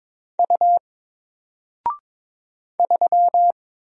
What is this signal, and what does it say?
Morse code, heard as a keyed tone, about 22 wpm, 700 Hz. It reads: U 3
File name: Single-Letter-Number-22wpm